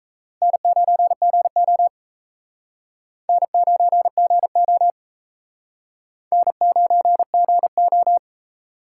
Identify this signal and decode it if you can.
Morse code, heard as a keyed tone, about 34 wpm, 700 Hz. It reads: N9GO N9GO N9GO
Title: Speed Racing - USA Call Signs 28wpm